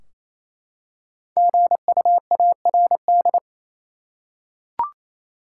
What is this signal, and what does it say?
Morse code, heard as a keyed tone, about 28 wpm, 700 Hz. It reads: GUARD
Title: Five Letter Words - 28wpm